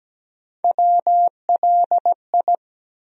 Morse code, heard as a keyed tone, about 17 wpm, 700 Hz. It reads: WLI
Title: USA Call Sign Postfixes - 3 Letters 17wpm